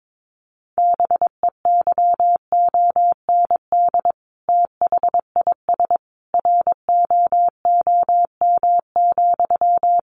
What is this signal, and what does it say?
Morse code, heard as a keyed tone, about 22 wpm, 700 Hz. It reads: BEYOND THIS ROOM,
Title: A Sound Of Thunder 22wpm